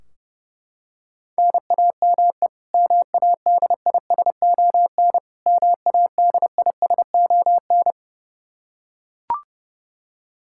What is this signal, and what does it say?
Morse code, heard as a keyed tone, about 32 wpm, 700 Hz. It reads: NAME MADISON MADISON
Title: K1USN SST QSO Practice 30wpm